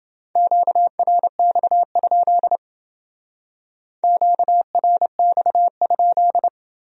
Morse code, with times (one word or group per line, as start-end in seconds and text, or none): 0.4-2.6 QRX?
4.0-6.5 QRX?